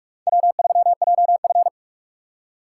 Morse code, heard as a keyed tone, about 43 wpm, 700 Hz. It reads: W3JF